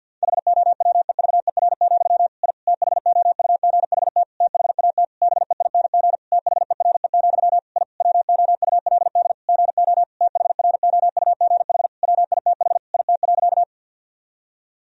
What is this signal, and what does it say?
Morse code, heard as a keyed tone, about 50 wpm, 700 Hz. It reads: HOWEVER, I THOUGHT THAT BEING THERE, I WOULD GO THROUGH WITH IT.
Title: Sentences from Top 1000 Words - Review Part 3 - 50wpm